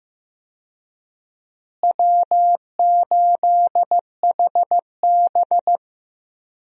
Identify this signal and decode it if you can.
Morse code, heard as a keyed tone, about 15 wpm, 700 Hz. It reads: W8HB
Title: USA Call Signs - 4 Characters - Repeated 3x - 15wpm